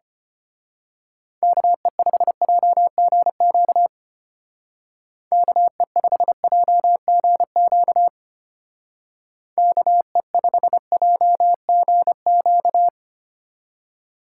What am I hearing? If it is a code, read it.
Morse code, heard as a keyed tone, about 31 wpm, 700 Hz. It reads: KE5JGQ KE5JGQ KE5JGQ
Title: Speed Racing - USA 2x3 Call Signs 25wpm